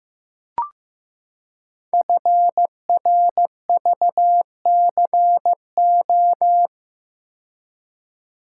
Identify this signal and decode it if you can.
Morse code, heard as a keyed tone, about 15 wpm, 700 Hz. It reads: FRVCO
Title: Code Groups - 5 Letters - 15wpm